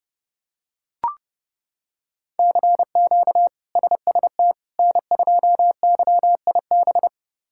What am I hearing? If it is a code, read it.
Morse code, heard as a keyed tone, about 30 wpm, 700 Hz. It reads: CQ SST N2YIB
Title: K1USN SST QSO Practice - 30wpm